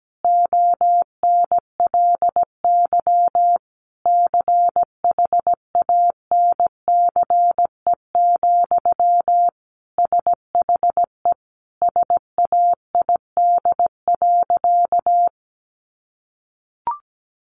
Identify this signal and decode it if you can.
Morse code, heard as a keyed tone, about 17 wpm, 700 Hz. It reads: ONLY CHANCE, SHE SAID.
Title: Sentences from Top 800 Words 17wpm